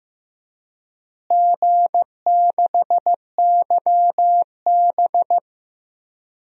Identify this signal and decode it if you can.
Morse code, heard as a keyed tone, about 15 wpm, 700 Hz. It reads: G6YB
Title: DXCC-100 Call Signs - 4 Characters - Repeated 3x - 15wpm